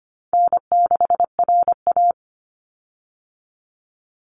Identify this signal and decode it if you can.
Morse code, heard as a keyed tone, about 25 wpm, 700 Hz. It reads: N6RA
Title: Easy Call Signs - 25wpm